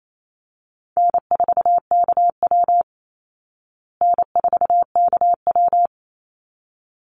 Morse code, heard as a keyed tone, about 28 wpm, 690 Hz. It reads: N4KW N4KW
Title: Call Signs Repeated 3x - 28wpm